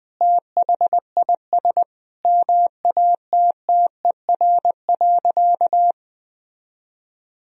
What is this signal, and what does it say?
Morse code, heard as a keyed tone, about 20 wpm, 700 Hz. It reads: THIS MATTER.
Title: Sentences from Top 1000 Words - Review Part 2 - 20wpm